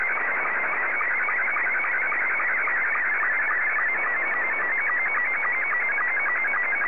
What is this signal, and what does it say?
SSTV (slow-scan television), Scottie 2.
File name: sstv / S2-JA8FFD